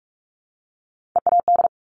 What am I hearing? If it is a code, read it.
Morse code, heard as a keyed tone, about 46 wpm, 700 Hz. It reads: EAD
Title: Top Three Letter Combinations in Words 45wpm